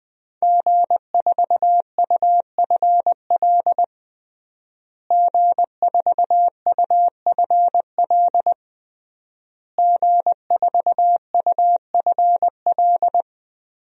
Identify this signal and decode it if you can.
Morse code, heard as a keyed tone, about 20 wpm, 700 Hz. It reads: G4UFL G4UFL G4UFL
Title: DXCC-100 Call Signs - 5 Characters - Repeated 3x - 20wpm